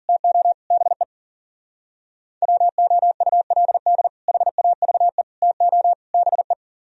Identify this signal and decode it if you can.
Morse code, heard as a keyed tone, about 50 wpm, 700 Hz. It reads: TO BE WOULD HAVE TO BE
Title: Speed-Racing - Sets of 4 Words 40wpm